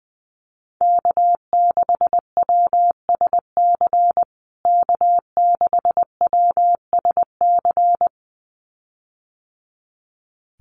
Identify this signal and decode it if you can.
Morse code, heard as a keyed tone, about 20 wpm, 700 Hz. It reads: K6WSC K6WSC
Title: USA Call Signs Repeated 2x 20wpm